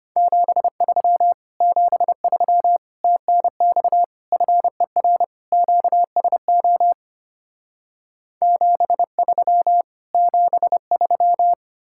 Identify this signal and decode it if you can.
Morse code, heard as a keyed tone, about 33 wpm, 700 Hz. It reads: 73 73 TNX FER QSO 73 73